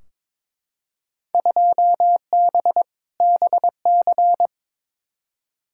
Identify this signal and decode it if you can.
Morse code, heard as a keyed tone, about 22 wpm, 700 Hz. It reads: 2B BC
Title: Field Day Exchanges Encore - 22wpm